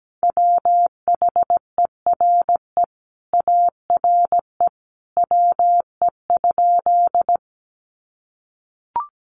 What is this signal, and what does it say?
Morse code, heard as a keyed tone, about 17 wpm, 700 Hz. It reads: WHERE ARE WE?
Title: Sentences from Top 200 Words 17wpm